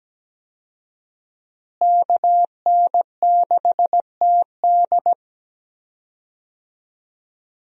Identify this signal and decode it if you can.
Morse code, heard as a keyed tone, about 17 wpm, 700 Hz. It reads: KN6TD
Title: USA Call Signs - 5 Characters - Repeated 2x - 17wpm